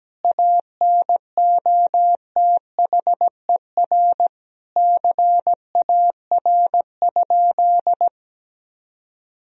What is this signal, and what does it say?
Morse code, heard as a keyed tone, about 17 wpm, 700 Hz. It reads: ANOTHER CAR?
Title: Sentences from Top 1000 Words - Review Part 1 - 17wpm